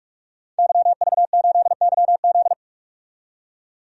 Morse code, heard as a keyed tone, about 46 wpm, 700 Hz. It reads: YV8YZ
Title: DXCC-100 Call Signs - 5 Characters - 45wpm